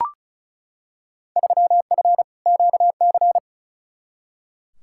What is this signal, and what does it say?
Morse code, heard as a keyed tone, about 36 wpm, 700 Hz. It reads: 3F QC
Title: Field Day Exchanges - 35wpm